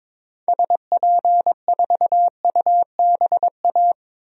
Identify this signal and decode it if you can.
Morse code, heard as a keyed tone, about 22 wpm, 700 Hz. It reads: SP4UBA
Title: DXCC-100 Call Signs - 6 Characters - Repeated 2x - 22wpm